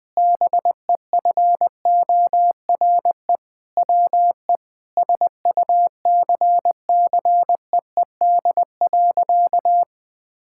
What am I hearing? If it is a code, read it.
Morse code, heard as a keyed tone, about 20 wpm, 700 Hz. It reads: BEFORE WE SUCCEED.
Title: Sentences with 7-Word Phrases - Spoken Beginning - 20wpm